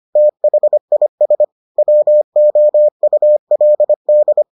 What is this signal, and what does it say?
Morse code, heard as a keyed tone, about 25 wpm, 600 Hz. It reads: THIS WOULD